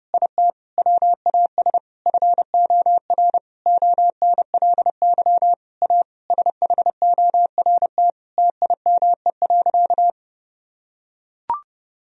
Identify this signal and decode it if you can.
Morse code, heard as a keyed tone, about 30 wpm, 700 Hz. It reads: IT WAS FOR ONLY A SHORT TIME.
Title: Sentences from Top 1000 Words - Review Part 3 - 30wpm